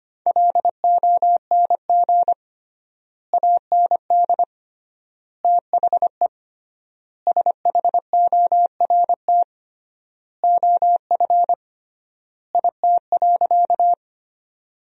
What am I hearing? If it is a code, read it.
Morse code, heard as a keyed tone, about 25 wpm, 700 Hz. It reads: LONG AND THE SHORT OF IT.
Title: 3x Word-Spacing - Sentences from Top 400 Words - 25wpm